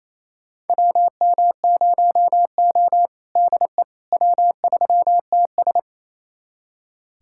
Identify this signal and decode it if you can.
Morse code, heard as a keyed tone, about 28 wpm, 700 Hz. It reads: WM0O DE W3TS
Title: USA Callsign DE Callsign - 4 Characters - 28wpm